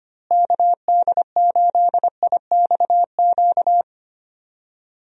Morse code, heard as a keyed tone, about 25 wpm, 700 Hz. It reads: KD8IXQ
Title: USA Call Signs - 6 Characters - Repeated 2x - 25wpm